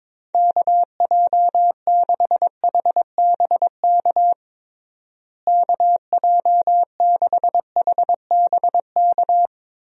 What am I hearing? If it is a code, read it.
Morse code, heard as a keyed tone, about 22 wpm, 700 Hz. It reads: KJ6HBK KJ6HBK
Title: USA Call Signs - 6 Characters - Repeated 3x - 22wpm